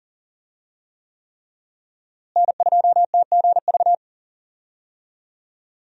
Morse code, heard as a keyed tone, about 40 wpm, 700 Hz. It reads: N2TGV
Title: USA Call Signs - 5 Characters - Repeated 3x - 40wpm